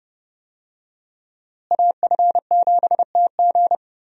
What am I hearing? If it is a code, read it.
Morse code, heard as a keyed tone, about 31 wpm, 700 Hz. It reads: AF7TG
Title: USA Call Signs - 5 Characters - Repeated 3x - 30wpm